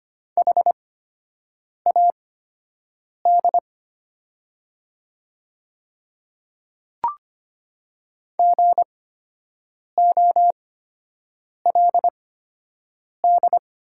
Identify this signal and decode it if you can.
Morse code, heard as a keyed tone, about 25 wpm, spaced as for 7 wpm, 700 Hz. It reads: HAD GOLD
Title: 8x Character Spacing - Top 500 Words - 25wpm